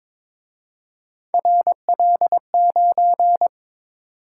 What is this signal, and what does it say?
Morse code, heard as a keyed tone, about 27 wpm, 700 Hz. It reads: RL9
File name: Speed-Racing - Three Letter-Number Combinations - 22wpm